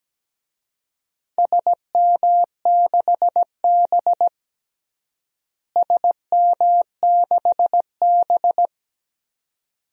Morse code, heard as a keyed tone, about 17 wpm, 700 Hz. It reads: SM6B SM6B
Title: DXCC-100 Call Signs - 4 Characters - Repeated 2x - 17wpm